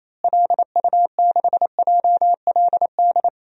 Morse code, heard as a keyed tone, about 28 wpm, 700 Hz. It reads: LU6JLD
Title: DXCC-100 Call Signs - 6 Characters - Repeated 3x - 28wpm